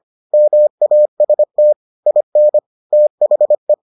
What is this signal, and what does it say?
Morse code, heard as a keyed tone, about 25 wpm, 600 Hz. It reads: MAST IN THE